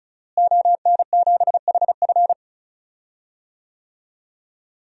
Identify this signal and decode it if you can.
Morse code, heard as a keyed tone, about 36 wpm, 700 Hz. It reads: ON7HF
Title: DXCC-100 Call Signs - 5 Characters - Repeated 3x - 35wpm